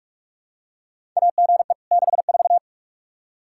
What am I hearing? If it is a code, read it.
Morse code, heard as a keyed tone, about 46 wpm, 700 Hz. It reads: AGE 64
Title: Practice QSO - Exchange 3 - More Info - 45wpm